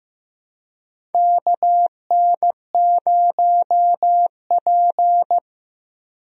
Morse code, heard as a keyed tone, about 15 wpm, 700 Hz. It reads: KN0P
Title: USA Call Signs - 4 Characters - 15wpm